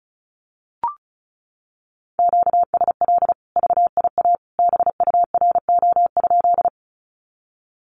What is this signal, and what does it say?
Morse code, heard as a keyed tone, about 34 wpm, 700 Hz. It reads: QSL VIA BURO?
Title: Common QSO Phrases 35wpm